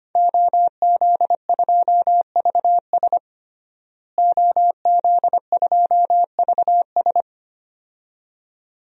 Morse code, heard as a keyed tone, about 25 wpm, 700 Hz. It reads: OZ2VS OZ2VS
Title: DXCC-100 Call Signs - 5 Characters - Repeated 2x - 25wpm